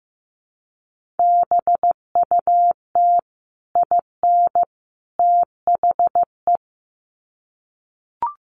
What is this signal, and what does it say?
Morse code, heard as a keyed tone, about 15 wpm, 700 Hz. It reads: BUT IN THE
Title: Sets of 3 Words 15wpm